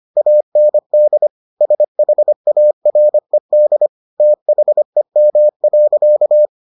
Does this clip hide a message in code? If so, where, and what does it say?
Yes: Morse code from 0.2 to 6.5 s, heard as a keyed tone, about 25 wpm, 600 Hz. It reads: AND SHARED THEM.